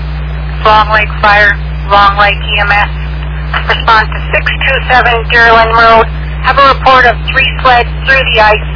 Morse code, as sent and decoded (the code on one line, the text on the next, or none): none